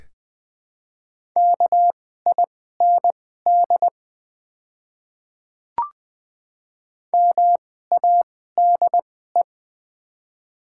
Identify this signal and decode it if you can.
Morse code, heard as a keyed tone, about 20 wpm, spaced as for 15 wpm, 700 Hz. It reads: KIND MADE
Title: 2x Character Spacing - Top 300 Words - 20wpm